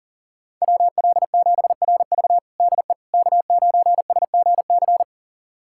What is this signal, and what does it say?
Morse code, heard as a keyed tone, about 40 wpm, 700 Hz. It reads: WL7RV DE K9SGC
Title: USA Callsign DE Callsign - 5 Characters - 40wpm